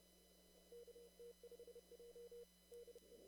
Morse code, heard as a keyed tone, about 31 wpm, 480 Hz. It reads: KT5J DE
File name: CW / kt5j wpx cw-op-k5pi